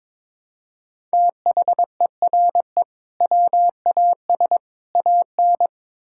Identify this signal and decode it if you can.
Morse code, heard as a keyed tone, about 22 wpm, 700 Hz. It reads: THERE WAS AN
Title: Sentences from Top 1000 Words - Review Part 6 - 22wpm